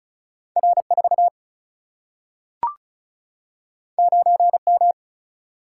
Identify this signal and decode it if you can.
Morse code, heard as a keyed tone, about 35 wpm, 700 Hz. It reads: R4 9M
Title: Single-Letter Single-Number 35wpm